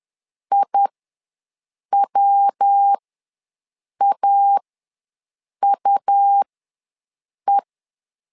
Morse code, heard as a keyed tone, about 10 wpm, spaced as for 6 wpm, 800 Hz. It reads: IWAUE
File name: LEZIONI / 3 MWRO / 3 MWR ws 10wpm 2151